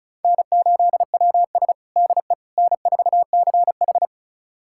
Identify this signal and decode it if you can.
Morse code, heard as a keyed tone, about 36 wpm, 700 Hz. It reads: N8WS DE N4CH
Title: USA Callsign DE Callsign - 4 Characters - 35wpm